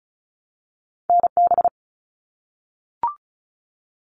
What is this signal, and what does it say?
Morse code, heard as a keyed tone, about 36 wpm, 700 Hz. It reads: NB
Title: Canadian Provinces and Territories 35wpm